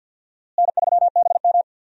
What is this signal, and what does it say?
Morse code, heard as a keyed tone, about 52 wpm, 700 Hz. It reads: N3BM
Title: USA Call Signs - 4 Characters - Repeated 2x - 50wpm